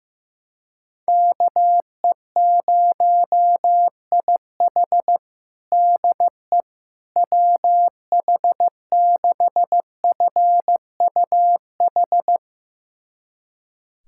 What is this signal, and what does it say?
Morse code, heard as a keyed tone, about 15 wpm, 700 Hz. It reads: KE0IH DE WH6FUH
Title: USA Callsign DE Callsign 15wpm